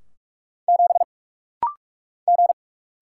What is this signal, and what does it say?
Morse code, heard as a keyed tone, about 46 wpm, 700 Hz. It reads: 7 G
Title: Single Letter-Number Alt - Rapid-Fire 45wpm